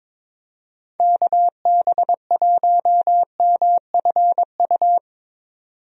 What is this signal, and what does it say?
Morse code, heard as a keyed tone, about 22 wpm, 700 Hz. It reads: KB1MFU
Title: USA Call Signs - 6 Characters - 22wpm